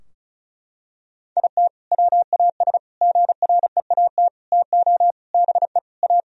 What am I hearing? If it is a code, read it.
Morse code, heard as a keyed tone, about 36 wpm, 700 Hz. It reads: IT WAS GREAT TO BE A
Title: Sentences from Top 1000 Words - Review Part 2 - 35wpm